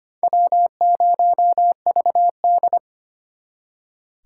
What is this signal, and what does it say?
Morse code, heard as a keyed tone, about 32 wpm, 700 Hz. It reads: W0VD